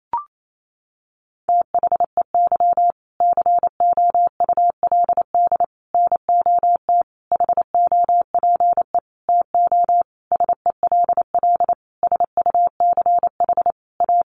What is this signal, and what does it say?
Morse code, heard as a keyed tone, about 28 wpm, 700 Hz. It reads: THEY COULD NOT HOPE TO SELL SUCH A
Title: Sentences from Top 1100 Words 28wpm